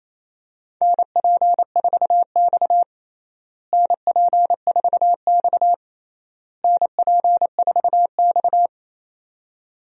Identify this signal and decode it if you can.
Morse code, heard as a keyed tone, about 28 wpm, 700 Hz. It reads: NP4X NP4X NP4X
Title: USA Call Signs - 4 Characters - Repeated 3x - 28wpm